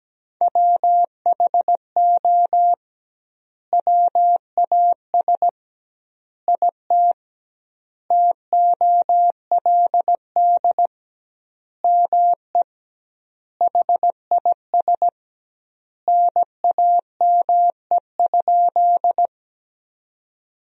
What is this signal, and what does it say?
Morse code, heard as a keyed tone, about 17 wpm, 700 Hz. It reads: WHO WAS IT TOLD ME HIS NAME?
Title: 2x Word-Spacing - Sentences from Top 400 Words - 17wpm